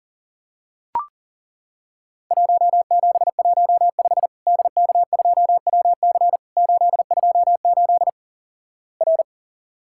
Morse code, heard as a keyed tone, about 42 wpm, 700 Hz. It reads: 1715 DK2WC 828 R
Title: Worked All Europe DX Contest - QTC Exchange - 40wpm